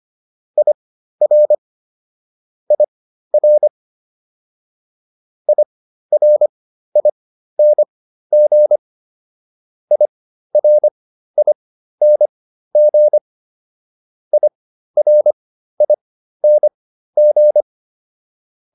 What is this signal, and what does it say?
Morse code, heard as a keyed tone, about 25 wpm, spaced as for 13 wpm, 600 Hz. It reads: IR IR IRING IRING IRING